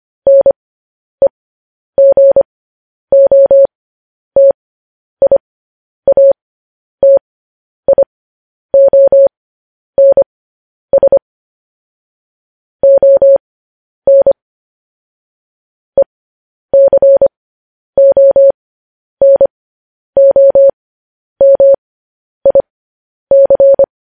Morse code, heard as a keyed tone, about 25 wpm, spaced as for 10 wpm, 550 Hz. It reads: NEGOTIATIONS ON ECONOMIC
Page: news-headlines-in-morse-code-at-10-wpm